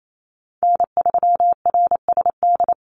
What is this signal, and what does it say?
Morse code, heard as a keyed tone, about 28 wpm, 700 Hz. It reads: N3RSD
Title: Call Signs Repeated 3x - 28wpm